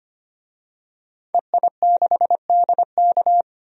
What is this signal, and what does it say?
Morse code, heard as a keyed tone, about 25 wpm, 700 Hz. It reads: EI6DK
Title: DXCC-100 Call Signs - 5 Characters - 25wpm